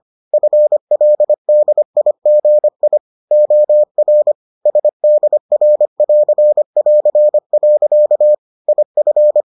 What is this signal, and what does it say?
Morse code, heard as a keyed tone, about 25 wpm, 600 Hz. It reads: FLDIGI OR SDR<AR><AR>. IF